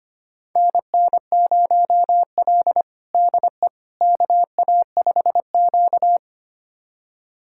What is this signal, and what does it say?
Morse code, heard as a keyed tone, about 25 wpm, 700 Hz. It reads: NN0L DE KA5Q
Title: USA Callsign DE Callsign - 4 Characters - 25wpm